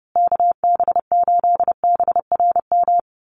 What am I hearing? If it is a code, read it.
Morse code, heard as a keyed tone, about 30 wpm, 700 Hz. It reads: KB8BRM
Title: USA Call Signs Repeated 2x 30wpm